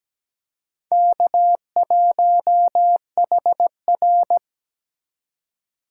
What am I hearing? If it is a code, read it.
Morse code, heard as a keyed tone, about 20 wpm, 700 Hz. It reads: K1HR
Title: Speed Racing - USA 1x2 Call Signs 17wpm